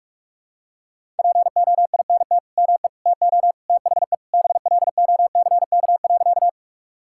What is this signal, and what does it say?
Morse code, heard as a keyed tone, about 48 wpm, 700 Hz. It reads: POINT ME TO THE BLOCK.
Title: Sentences from Top 1600 Words 45wpm